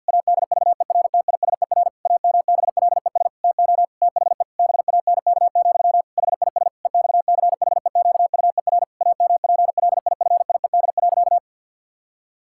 Morse code, heard as a keyed tone, about 50 wpm, 700 Hz. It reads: ADVERTISER AMBLES TO THE BANK, HIS EXCHEQUER AMPLIFIED.
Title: Pangram Challenge v2 - 50wpm